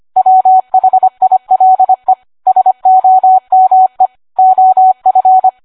Morse code, heard as a keyed tone, about 25 wpm, 750 Hz. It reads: WHILE SOME OF